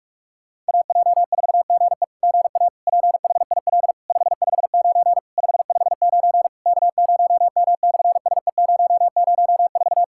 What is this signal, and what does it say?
Morse code, heard as a keyed tone, about 46 wpm, 690 Hz. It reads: AJ4GE GA PHIL 559 559 K0M/SE004
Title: SOTA QSO Practice 45wpm